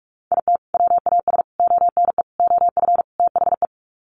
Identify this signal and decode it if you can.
Morse code, heard as a keyed tone, about 44 wpm, 700 Hz. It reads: IT WAS ONE OF THE
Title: Sets of 5 Words 45wpm